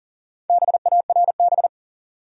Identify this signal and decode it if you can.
Morse code, heard as a keyed tone, about 37 wpm, 700 Hz. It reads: BARB
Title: USA Common Names - Max 4 Letters - 40wpm